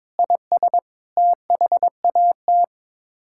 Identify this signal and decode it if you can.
Morse code, heard as a keyed tone, about 22 wpm, 700 Hz. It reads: IS THAT
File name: Sets of 9 Words - 22wpm